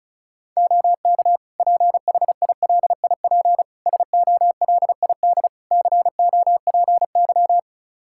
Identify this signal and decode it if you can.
Morse code, heard as a keyed tone, about 36 wpm, 700 Hz. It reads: OK PHILIP SOLID COPY
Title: K1USN SST QSO Practice 35wpm